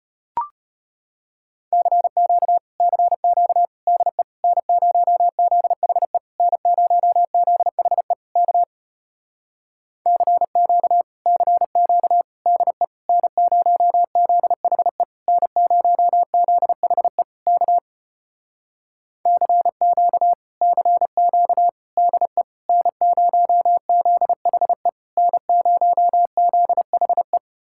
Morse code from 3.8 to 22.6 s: DE N0ZHE N0ZHE K CQ CQ DE N0ZHE N0ZHE K CQ CQ DE

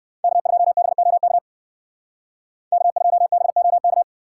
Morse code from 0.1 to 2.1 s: K2BOK

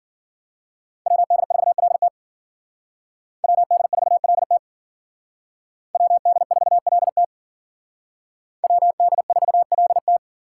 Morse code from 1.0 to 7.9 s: WD4LT WD4LT WD4LT

